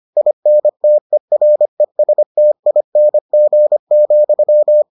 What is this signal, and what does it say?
Morse code, heard as a keyed tone, about 25 wpm, 600 Hz. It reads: INTERESTING,